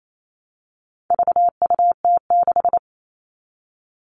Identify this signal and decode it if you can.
Morse code, heard as a keyed tone, about 28 wpm, 700 Hz. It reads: VUT6
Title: All Cut Numbers - 4-Digit 28wpm